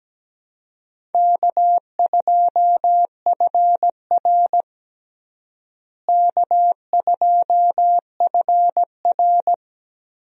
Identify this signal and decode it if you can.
Morse code, heard as a keyed tone, about 17 wpm, 700 Hz. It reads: K2FR K2FR
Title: USA Call Signs - 4 Characters - Repeated 2x - 17wpm